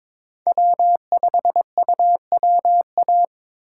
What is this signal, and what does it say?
Morse code, heard as a keyed tone, about 22 wpm, 700 Hz. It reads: W5UWA
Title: USA Call Signs - 5 Characters - 3x Character Spacing - 22wpm